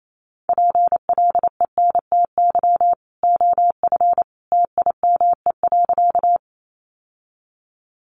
Morse code, heard as a keyed tone, about 28 wpm, 700 Hz. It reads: PLENTY OF TIME.
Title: Sentences from Top 1300 Words 28wpm